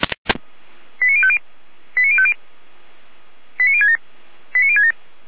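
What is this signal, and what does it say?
Five-tone selective call (ZVEI), heard as quick runs of five tones, tones of about 70 ms. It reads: 89050 89050 89067 89067
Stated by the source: Five-tone ZVEI selective call, SDR receiver, 455 channel.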